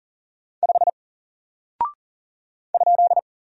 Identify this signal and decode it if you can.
Morse code, heard as a keyed tone, about 40 wpm, 700 Hz. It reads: 5 ?
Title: Single Character (Letter:Number:Punctuation:Prosign) 40wpm